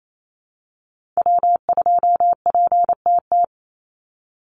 Morse code, heard as a keyed tone, about 28 wpm, 700 Hz. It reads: W2PTT
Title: Call Signs Repeated 3x - 28wpm